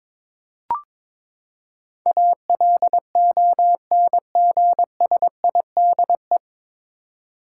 Morse code, heard as a keyed tone, about 22 wpm, 700 Hz. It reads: ALONGSIDE
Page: Morse Code Ninja practice 22wpm